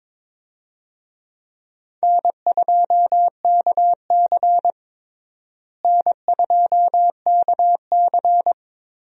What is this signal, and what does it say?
Morse code, heard as a keyed tone, about 21 wpm, 700 Hz. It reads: N2KC N2KC
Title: USA Call Signs - 4 Characters - Repeated 3x - 22wpm